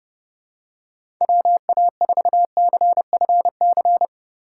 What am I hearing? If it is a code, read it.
Morse code, heard as a keyed tone, about 32 wpm, 700 Hz. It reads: WA4CFC
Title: USA Call Signs - 6 Characters - 3x Character Spacing - 30wpm